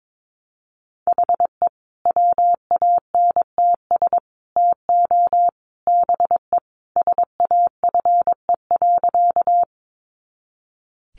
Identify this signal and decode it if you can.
Morse code, heard as a keyed tone, about 22 wpm, 700 Hz. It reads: HE WANTS TO BE SAFE.
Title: Additional Easy Phrases 22wpm